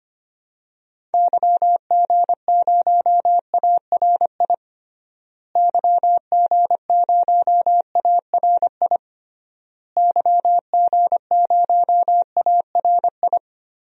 Morse code, heard as a keyed tone, about 25 wpm, 700 Hz. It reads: YG0ARI YG0ARI YG0ARI
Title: DXCC-100 Call Signs - 6 Characters - Repeated 3x - 25wpm